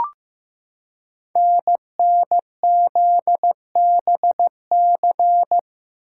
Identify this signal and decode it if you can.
Morse code, heard as a keyed tone, about 15 wpm, 700 Hz. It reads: NNZBC
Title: Code Groups - 5 Letters - 15wpm